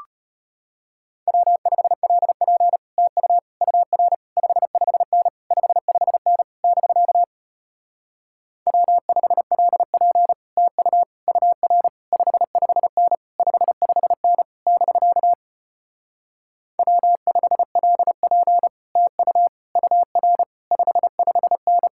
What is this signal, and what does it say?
Morse code, heard as a keyed tone, about 33 wpm, 700 Hz. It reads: W5LP TU UR 55N 55N <BK> W5LP TU UR 55N 55N <BK> W5LP TU UR 55N